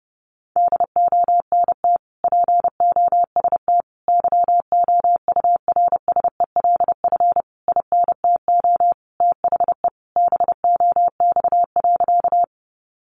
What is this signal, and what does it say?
Morse code, heard as a keyed tone, about 30 wpm, 700 Hz. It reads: DONT POST YOURSELF INTO THE BOX.
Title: Sentences from Top 1000 Words 30wpm